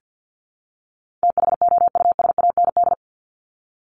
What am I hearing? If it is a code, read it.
Morse code, heard as a keyed tone, about 50 wpm, 700 Hz. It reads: THOUSAND
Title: Top 500 Words 50wpm